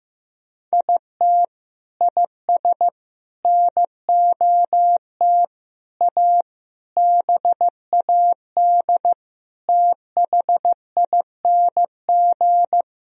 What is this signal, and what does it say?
Morse code, heard as a keyed tone, about 15 wpm, 700 Hz. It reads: IT IS NOT A BAD THING
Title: Sentences from Top 1000 Words - Review Part 4 - 15wpm